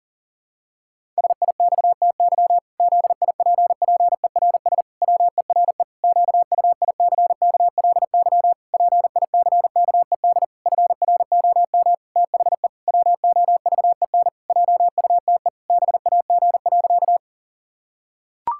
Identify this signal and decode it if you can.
Morse code, heard as a keyed tone, about 40 wpm, 700 Hz. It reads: SIXTY ZIPPERS WERE QUICKLY PICKED FROM THE WOVEN JUTE BAG.
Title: Pangram Challenge v2 - 40wpm